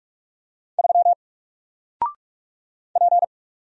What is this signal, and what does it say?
Morse code, heard as a keyed tone, about 46 wpm, 700 Hz. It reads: 3 P
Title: Single Letter-Number 45wpm